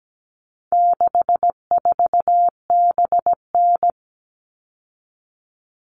Morse code, heard as a keyed tone, about 17 wpm, 700 Hz. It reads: 64BN
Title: All Cut Numbers - 4-Digit 17wpm